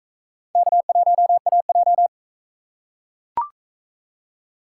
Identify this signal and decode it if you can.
Morse code, heard as a keyed tone, about 43 wpm, 700 Hz. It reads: K1AJ